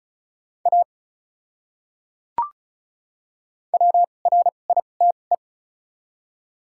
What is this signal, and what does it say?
Morse code, heard as a keyed tone, about 36 wpm, spaced as for 25 wpm, 700 Hz. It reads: A WRITE
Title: 2x Character Spacing - Top 100 Words - 35wpm